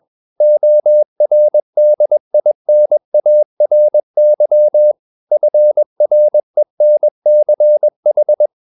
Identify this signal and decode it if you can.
Morse code, heard as a keyed tone, about 21 wpm, 600 Hz. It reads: ORDINARY FRENCH
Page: CW med Gnister